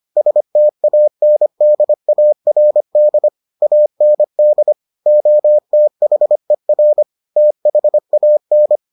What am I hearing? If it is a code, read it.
Morse code, heard as a keyed tone, about 25 wpm, 600 Hz. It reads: STANDARD AND OTHER THAN